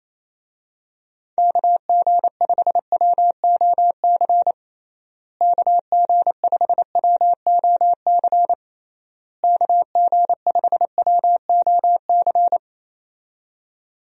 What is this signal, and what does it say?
Morse code, heard as a keyed tone, about 27 wpm, 700 Hz. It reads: KG5WOC KG5WOC KG5WOC
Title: USA Call Signs - 6 Characters - Repeated 3x - 28wpm